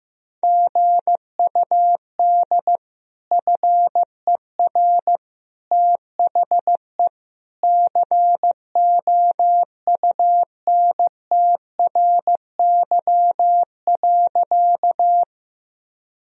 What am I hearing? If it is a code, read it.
Morse code, heard as a keyed tone, about 15 wpm, 700 Hz. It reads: GUD FER THE COUNTRY.
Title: Sentences from Top 300 Words and Abbreviations 15wpm